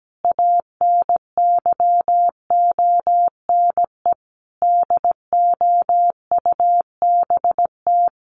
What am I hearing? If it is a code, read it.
Morse code, heard as a keyed tone, about 17 wpm, 700 Hz. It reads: ANYONE DOUBT
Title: Sentences from Top 1500 Words 17wpm